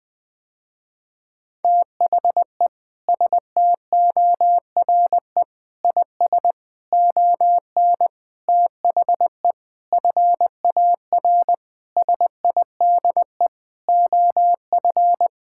Morse code, heard as a keyed tone, about 20 wpm, 700 Hz. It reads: THE STORE IS ON THE FAR SIDE OF
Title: Sentences with 6-Word Phrases - Spoken Beginning - 20wpm